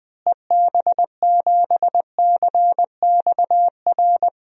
Morse code, heard as a keyed tone, about 20 wpm, 700 Hz. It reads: EB7CXR
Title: DXCC-100 Call Signs - 6 Characters - Repeated 3x - 20wpm